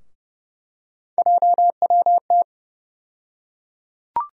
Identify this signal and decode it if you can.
Morse code, heard as a keyed tone, about 30 wpm, 700 Hz. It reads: JWT
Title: USA Call Sign Postfixes - 3 Letters 30wpm